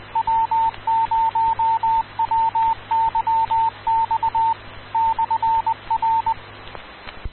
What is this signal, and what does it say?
Morse code, heard as a keyed tone, about 20 wpm, 900 Hz, with radio noise. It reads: W0WYX /R
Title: Rocky Mountain Radio League Repeaters